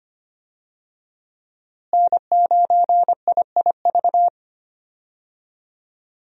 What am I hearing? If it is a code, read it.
Morse code, heard as a keyed tone, about 25 wpm, 700 Hz. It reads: N9IIV
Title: USA Call Signs - 5 Characters - Repeated 2x - 25wpm